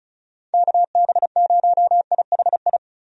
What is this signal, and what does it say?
Morse code, heard as a keyed tone, about 34 wpm, 700 Hz. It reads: KB0IHI
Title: USA Call Signs - 6 Characters - 35wpm